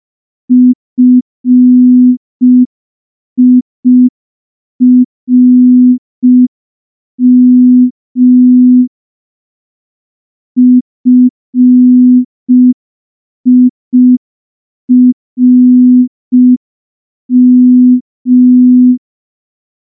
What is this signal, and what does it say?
Morse code, heard as a keyed tone, about 5 wpm, 250 Hz. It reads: FIRM FIRM
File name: morse / morse-training-05WPM